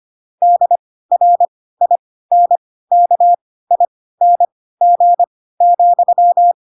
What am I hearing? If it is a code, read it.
Morse code, heard as a keyed tone, about 25 wpm, spaced as for 16 wpm, 700 Hz. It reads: DRINKING,
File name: Pinocchio / Pinocchio ch4 10 25 ik1hge